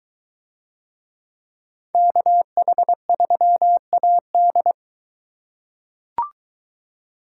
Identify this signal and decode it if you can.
Morse code, heard as a keyed tone, about 22 wpm, 700 Hz. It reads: KH3AD